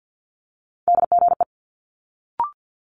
Morse code, heard as a keyed tone, about 50 wpm, 700 Hz. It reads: DGE
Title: Multi-letter Phonemes Quick Fire 50wpm